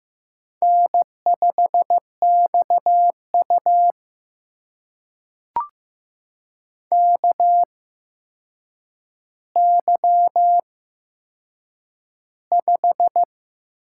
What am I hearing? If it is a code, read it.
Morse code, heard as a keyed tone, about 15 wpm, spaced as for 7 wpm, 700 Hz. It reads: N5XU KY5
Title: USA Call Signs - 4 Characters - 8x Character Spacing - 15wpm